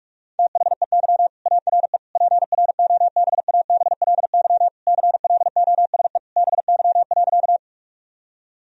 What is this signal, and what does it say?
Morse code, heard as a keyed tone, about 46 wpm, 700 Hz. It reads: THEY ARE PROBABLY CLOSE BY.
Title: Sentences from Top 1000 Words - Review Part 4 - 45wpm